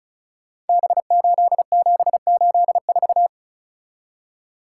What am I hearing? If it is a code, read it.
Morse code, heard as a keyed tone, about 36 wpm, 700 Hz. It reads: B8784
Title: Code Groups - 5 Letter:Numbers - 35wpm